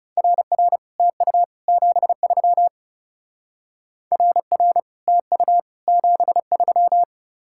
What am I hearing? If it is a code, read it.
Morse code, heard as a keyed tone, about 38 wpm, 700 Hz. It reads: RR TU 73 RR TU 73